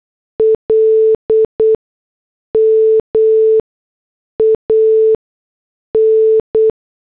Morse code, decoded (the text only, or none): LMAN